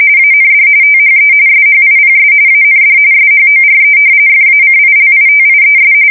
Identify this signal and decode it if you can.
RTTY, about 45.45 baud. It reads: AND THE SHIPS AND SHORE FACILITIES
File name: fsk170 / usnavy3